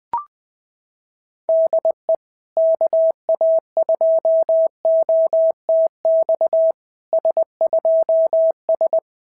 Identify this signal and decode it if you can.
Morse code, heard as a keyed tone, about 21 wpm, 650 Hz. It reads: DE KA2OTX S2S
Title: SOTA QSO Practice 20wpm